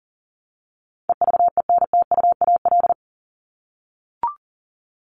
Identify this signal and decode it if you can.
Morse code, heard as a keyed tone, about 40 wpm, 700 Hz. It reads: EVENTUAL
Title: LY Suffix - Top 100 Words with Base Words 40wpm